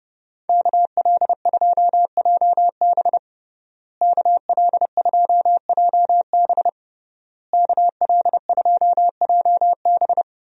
Morse code, heard as a keyed tone, about 30 wpm, 700 Hz. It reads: KL2JB KL2JB KL2JB
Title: USA Call Signs - 5 Characters - Repeated 3x - 30wpm